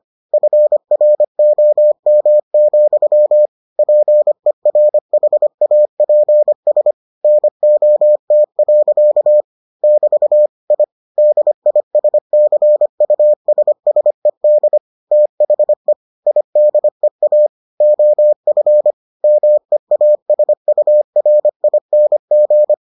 Morse code, heard as a keyed tone, about 25 wpm, 600 Hz. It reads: FROM, PERHAPS NOT. <BT> I DISCUSSED THE IDEA OF MEASURING